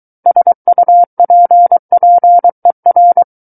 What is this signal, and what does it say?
Morse code, heard as a keyed tone, about 23 wpm, 700 Hz. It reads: SUPPER
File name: Book One / 14-Chapter 14 - 23 WPM